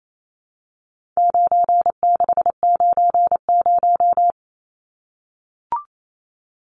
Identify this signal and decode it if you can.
Morse code, heard as a keyed tone, about 28 wpm, 700 Hz. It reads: 9690
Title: Numbers - 4-Digit 28wpm